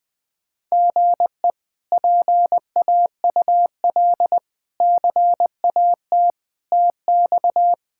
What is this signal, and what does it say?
Morse code, heard as a keyed tone, about 20 wpm, 700 Hz. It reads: GE PAUL CAT TX
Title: K1USN SST QSO Practice - 20wpm